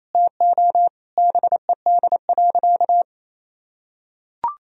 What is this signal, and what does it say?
Morse code, heard as a keyed tone, about 28 wpm, 700 Hz. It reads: TO BED.
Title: Sentences from Top 1000 Words - Review Part 3 - 28wpm